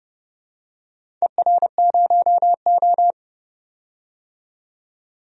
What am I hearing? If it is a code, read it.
Morse code, heard as a keyed tone, about 28 wpm, 700 Hz. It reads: ER0O
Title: DXCC-100 Call Signs - 4 Characters - 30wpm